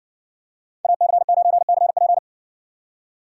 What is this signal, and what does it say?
Morse code, heard as a keyed tone, about 60 wpm, 700 Hz. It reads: AC9CP